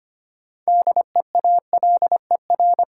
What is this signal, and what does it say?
Morse code, heard as a keyed tone, about 25 wpm, 700 Hz. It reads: DEALER
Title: Sentences from Top 1100 Words - New Word Drill Alt - 25wpm